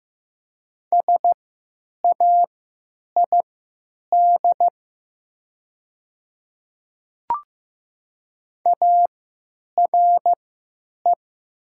Morse code, heard as a keyed tone, about 15 wpm, spaced as for 9 wpm, 700 Hz. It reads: SAID ARE
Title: 3x Character Spacing - Top 300 Words - 15wpm